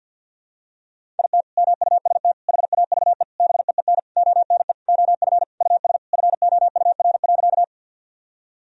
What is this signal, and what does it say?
Morse code, heard as a keyed tone, about 47 wpm, 700 Hz. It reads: IT MUST HAVE BEEN ONE OF US FOUR.
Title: Sentences from Top 1000 Words - Review Part 1 - 50wpm